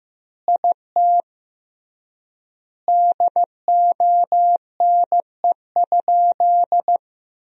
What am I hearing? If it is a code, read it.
Morse code, heard as a keyed tone, about 14 wpm, 700 Hz. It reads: IT DONE?
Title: 3x Word-Spacing - Sentences from Top 500 Words - 15wpm